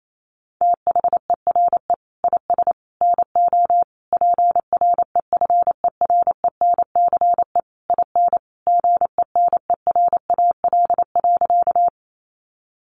Morse code, heard as a keyed tone, about 28 wpm, 700 Hz. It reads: THERE IS NO PREFERENCE IN GENERAL.
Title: Sentences from Top 1500 Words 28wpm